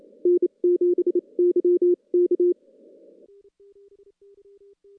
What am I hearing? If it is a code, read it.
Morse code, heard as a keyed tone, about 28 wpm, spaced as for 24 wpm, 350 Hz. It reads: N7YK
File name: contests / CW-SS-k5tr- -2300